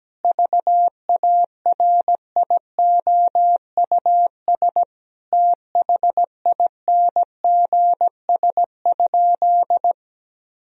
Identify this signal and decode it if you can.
Morse code, heard as a keyed tone, about 17 wpm, 700 Hz. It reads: VARIOUS THINGS?
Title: Sentences from Top 1000 Words - Review Part 4 - 17wpm